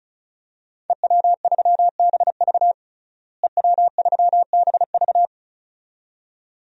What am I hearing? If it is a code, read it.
Morse code, heard as a keyed tone, about 36 wpm, 700 Hz. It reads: EW3BV EW3BV
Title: DXCC-100 Call Signs - 5 Characters - Repeated 2x - 35wpm